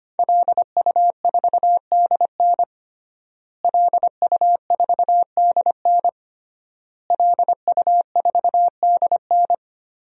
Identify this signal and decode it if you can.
Morse code, heard as a keyed tone, about 25 wpm, 700 Hz. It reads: LU4DN LU4DN LU4DN
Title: DXCC-100 Call Signs - 5 Characters - Repeated 3x - 25wpm